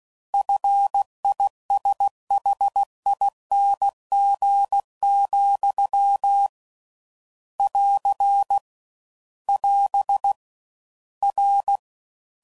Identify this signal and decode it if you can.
Morse code, heard as a keyed tone, about 16 wpm, 800 Hz. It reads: FISHING, <AR> <AS> R